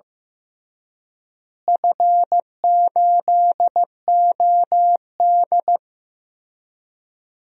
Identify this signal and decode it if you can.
Morse code, heard as a keyed tone, about 15 wpm, 700 Hz. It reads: F8OD
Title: DXCC-100 Call Signs - 4 Characters - Repeated 3x - 15wpm